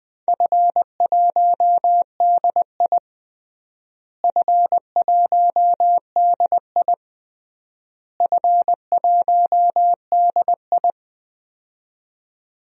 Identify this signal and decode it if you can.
Morse code, heard as a keyed tone, about 20 wpm, 700 Hz. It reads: F1DI F1DI F1DI
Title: DXCC-100 Call Signs - 4 Characters - Repeated 3x - 20wpm